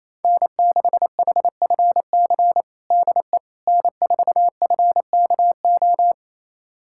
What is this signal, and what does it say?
Morse code, heard as a keyed tone, about 27 wpm, 700 Hz. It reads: N6HFC DE N4FKO
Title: USA Callsign DE Callsign - 5 Characters - 28wpm